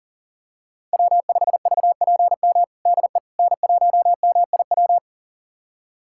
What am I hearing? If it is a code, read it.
Morse code, heard as a keyed tone, about 37 wpm, 700 Hz. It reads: W5VPM DE N1MIW
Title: USA Callsign DE Callsign - 5 Characters - 40wpm